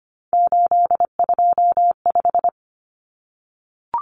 Morse code, heard as a keyed tone, about 25 wpm, 700 Hz. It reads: 825
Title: Common Cut Numbers - 3-Digit 25wpm